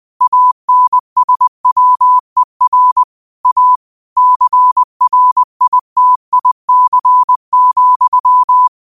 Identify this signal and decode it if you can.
Morse code, heard as a keyed tone, about 20 wpm, 1000 Hz. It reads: ANSWER A CRITIC,
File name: 20wpm